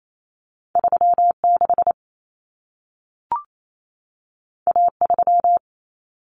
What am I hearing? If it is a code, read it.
Morse code, heard as a keyed tone, about 28 wpm, 700 Hz. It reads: 36 A3
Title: Common Cut Numbers - 2-Digit 28wpm